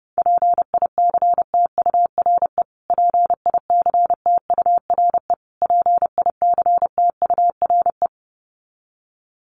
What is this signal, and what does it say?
Morse code, heard as a keyed tone, about 30 wpm, 700 Hz. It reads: PICTURE PICTURE PICTURE
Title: Top 500 Words Repeated 30wpm